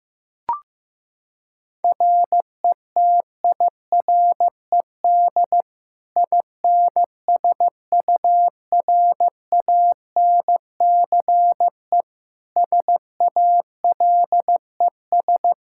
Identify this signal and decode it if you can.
Morse code, heard as a keyed tone, about 15 wpm, 700 Hz. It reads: RETIRED INSURANCE SALES
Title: Practice QSO - Exchange 3 - More Info - 15wpm